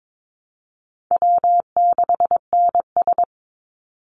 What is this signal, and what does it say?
Morse code, heard as a keyed tone, about 22 wpm, 700 Hz. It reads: W6NS
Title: Easy Call Signs - 22wpm